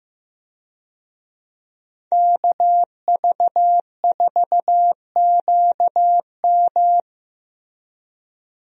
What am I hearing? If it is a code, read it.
Morse code, heard as a keyed tone, about 15 wpm, 700 Hz. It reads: KV4QM
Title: USA Call Signs - 5 Characters - Repeated 2x - 15wpm